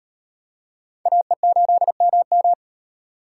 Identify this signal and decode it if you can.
Morse code, heard as a keyed tone, about 38 wpm, 700 Hz. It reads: AE8MM